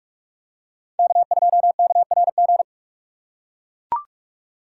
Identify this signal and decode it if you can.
Morse code, heard as a keyed tone, about 44 wpm, 700 Hz. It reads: K2KRG
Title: USA Call Signs - 5 Characters - 6x Character Spacing - 45wpm